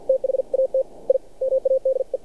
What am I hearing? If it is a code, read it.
Morse code, heard as a keyed tone, about 53 wpm, spaced as for 47 wpm, 530 Hz. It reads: THAT I MADE
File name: archived / CRSnet